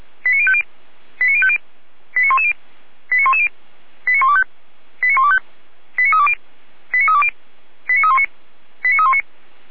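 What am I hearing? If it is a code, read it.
Five-tone selective call (ZVEI), heard as quick runs of five tones, tones of about 70 ms. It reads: 89050 89050 89110 89110 89125 89125 89320 89320 89319 89319
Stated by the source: ZVEI five-tone selective call received on SDR.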